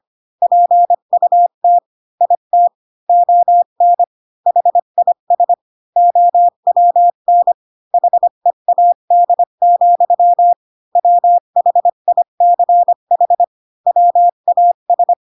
Morse code, heard as a keyed tone, about 25 wpm, 700 Hz. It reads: PUT IT ON HIS OWN HEAD, WHICH WAS
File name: Pinocchio / Pinocchio ch3 20 25 ik1hge